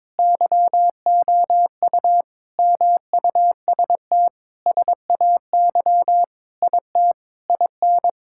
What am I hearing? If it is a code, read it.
Morse code, heard as a keyed tone, about 22 wpm, 700 Hz. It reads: YOU MUST SAY IT IN
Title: Sentences from Top 1600 Words 22wpm